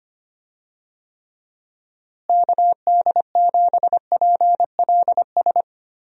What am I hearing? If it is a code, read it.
Morse code, heard as a keyed tone, about 25 wpm, 700 Hz. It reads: KD7PLS
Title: USA Call Signs - 6 Characters - Repeated 2x - 25wpm